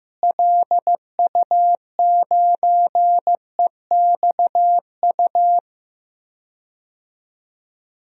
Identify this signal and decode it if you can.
Morse code, heard as a keyed tone, about 15 wpm, 700 Hz. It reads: LU9EXU
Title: DXCC-100 Call Signs - 6 Characters - Repeated 3x - 15wpm